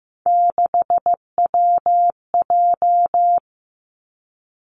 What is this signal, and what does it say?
Morse code, heard as a keyed tone, about 15 wpm, 700 Hz. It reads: 6WJ
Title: Three Challenging Letter-Number Combinations 15wpm